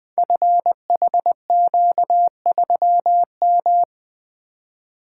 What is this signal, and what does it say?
Morse code, heard as a keyed tone, about 20 wpm, 700 Hz. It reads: FHQ3M
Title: Code Groups - 5 Letter:Numbers - 20wpm